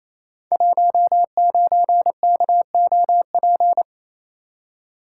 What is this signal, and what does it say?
Morse code, heard as a keyed tone, about 27 wpm, 700 Hz. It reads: 19KOP
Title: Code Groups - 5 Letter:Numbers - 28wpm